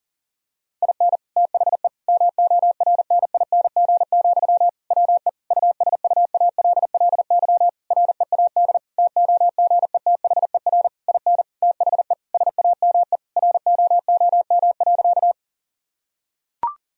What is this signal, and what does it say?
Morse code, heard as a keyed tone, about 37 wpm, 700 Hz. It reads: IN THE MORNING, WE USUALLY READ TOGETHER IN THE SAME ROOM.
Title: Sentences from Top 1000 Words - Review Part 3 - 40wpm